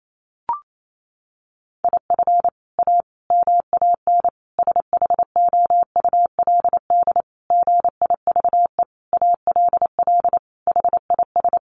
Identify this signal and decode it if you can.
Morse code, heard as a keyed tone, about 28 wpm, 700 Hz. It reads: IF A MAN SHOULD GIVE ALL HIS
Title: Sentences from Top 1500 Words 28wpm